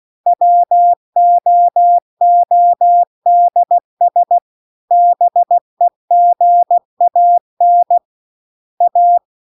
Morse code, heard as a keyed tone, about 16 wpm, 700 Hz. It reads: WOODS BEGAN A